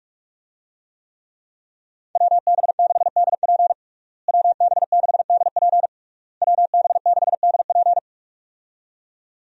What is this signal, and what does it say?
Morse code, heard as a keyed tone, about 46 wpm, 700 Hz. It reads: WB6DP WB6DP WB6DP
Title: USA Call Signs - 5 Characters - Repeated 3x - 45wpm